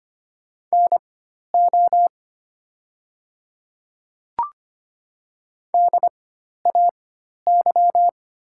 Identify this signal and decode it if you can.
Morse code, heard as a keyed tone, about 25 wpm, spaced as for 12 wpm, 700 Hz. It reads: NO DAY
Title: 4x Character Spacing - Top 200 Words - 25wpm